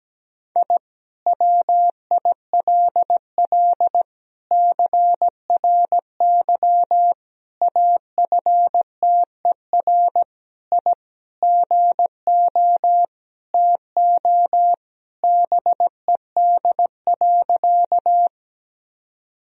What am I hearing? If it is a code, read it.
Morse code, heard as a keyed tone, about 17 wpm, 700 Hz. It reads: I WILL CRY AFTER I GO TO BED.
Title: Sentences from Top 1000 Words - Review Part 2 - 17wpm